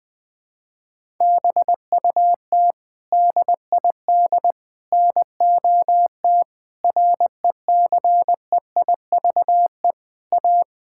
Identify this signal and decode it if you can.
Morse code, heard as a keyed tone, about 20 wpm, 700 Hz. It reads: BUT DID NOT RECEIVE A
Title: Sets of 8 Words - 20wpm